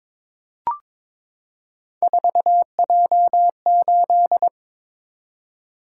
Morse code, heard as a keyed tone, about 22 wpm, 700 Hz. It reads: 4J8
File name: practice / 22wpm / Three Letter-Number Combinations - 22wpm